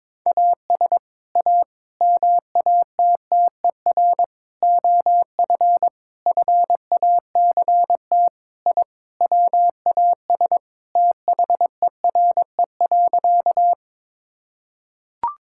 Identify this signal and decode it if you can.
Morse code, heard as a keyed tone, about 21 wpm, 700 Hz. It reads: AS A MATTER OF FACT I WAS THERE.
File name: practice / 22wpm / Sentences with 4-Word Phrases - 22wpm